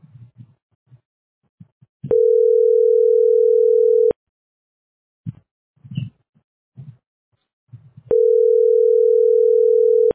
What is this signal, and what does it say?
Ringback tone (North America), 440 + 480 Hz.